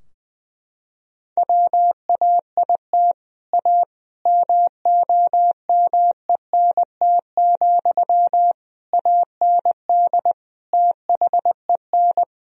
Morse code, heard as a keyed tone, about 20 wpm, 700 Hz. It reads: WAIT A MOMENT, AND THEN
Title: Sentences from Top 1000 Words - Review Part 4 - 20wpm